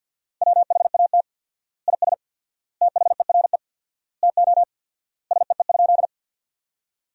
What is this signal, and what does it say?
Morse code, heard as a keyed tone, about 48 wpm, 700 Hz. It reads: WHAT IS THERE TO SEE?
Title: 4x Word-Spacing - Sentences from Top 100 Words - 50wpm